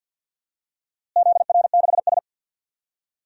Morse code, heard as a keyed tone, about 47 wpm, 700 Hz. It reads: ZR6S
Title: DXCC-100 Call Signs - 4 Characters - Repeated 3x - 50wpm